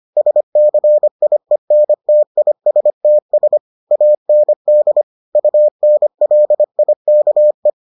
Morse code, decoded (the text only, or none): SCIENTISTS AND UNLIKE